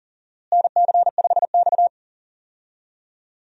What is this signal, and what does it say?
Morse code, heard as a keyed tone, about 39 wpm, 700 Hz. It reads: NC5X
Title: USA Call Signs - 4 Characters - Repeated 3x - 40wpm